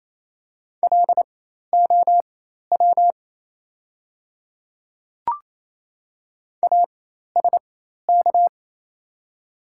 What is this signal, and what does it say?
Morse code, heard as a keyed tone, about 28 wpm, spaced as for 13 wpm, 690 Hz. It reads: LOW ASK
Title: 4x Character Spacing - Top 200 Words - 28wpm